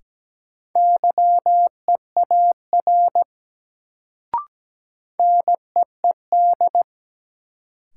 Morse code, heard as a keyed tone, about 17 wpm, 700 Hz. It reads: YEAR NEED
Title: Sentences from Top 200 Words - New Word Drill - 17wpm